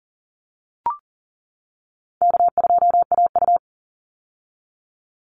Morse code, heard as a keyed tone, about 40 wpm, 700 Hz. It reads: K2AU
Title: Call Signs - 40wpm